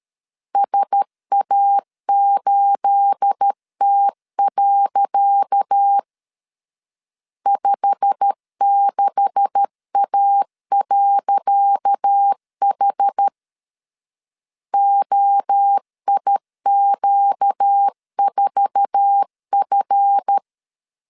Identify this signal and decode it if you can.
Morse code, heard as a keyed tone, about 13 wpm, 800 Hz. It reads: SA8T. 56A.H OIQ4F
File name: LEZIONI / 10 SIGNS / 10C DOT 12wpm 2451